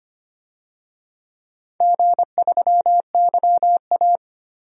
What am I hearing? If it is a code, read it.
Morse code, heard as a keyed tone, about 25 wpm, 700 Hz. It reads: G3YA
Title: DXCC-100 Call Signs - 4 Characters - 25wpm